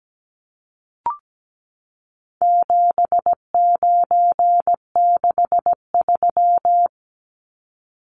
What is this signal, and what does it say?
Morse code, heard as a keyed tone, about 17 wpm, 700 Hz. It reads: 7963
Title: Numbers - 4-Digit 17wpm